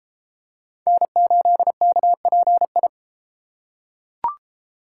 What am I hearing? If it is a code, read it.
Morse code, heard as a keyed tone, about 32 wpm, 700 Hz. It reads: N8KPI